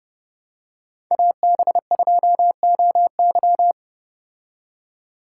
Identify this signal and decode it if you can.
Morse code, heard as a keyed tone, about 29 wpm, 700 Hz. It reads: AB2OY